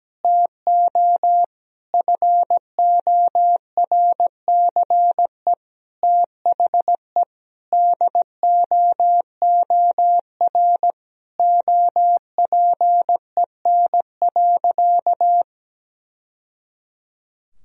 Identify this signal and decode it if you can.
Morse code, heard as a keyed tone, about 17 wpm, 700 Hz. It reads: TO FORCE THE DOOR OPEN.
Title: Sentences from Top 1000 Words - Review Part 3 - 17wpm